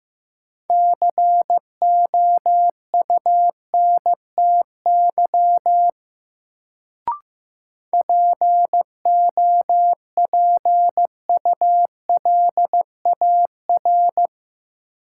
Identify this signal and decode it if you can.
Morse code, heard as a keyed tone, about 15 wpm, 700 Hz. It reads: COUNTY POPULAR
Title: Sentences from Top 1200 Words - New Word Drill Alt - 15wpm